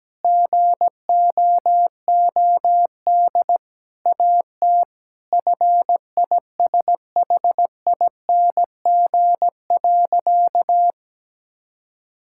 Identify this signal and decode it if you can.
Morse code, heard as a keyed tone, about 17 wpm, 700 Hz. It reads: GOOD AT FISHING.
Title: Sentences from Top 1000 Words - Review Part 5 - 17wpm